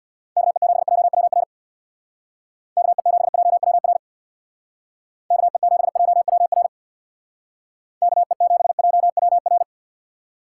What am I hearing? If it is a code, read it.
Morse code, heard as a keyed tone, about 63 wpm, 700 Hz. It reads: KE7JWR KE7JWR KE7JWR KE7JWR